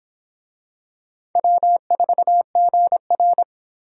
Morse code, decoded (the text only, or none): W4GR